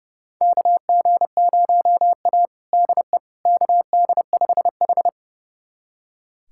Morse code, heard as a keyed tone, about 30 wpm, 700 Hz. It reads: KG0A DE KD5H
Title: USA Callsign DE Callsign - 4 Characters - 30wpm